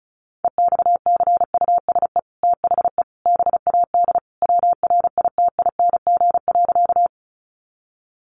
Morse code, heard as a keyed tone, about 34 wpm, 700 Hz. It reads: EXCUSE THE BAD WRITING.
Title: Sentences from Top 800 Words 35wpm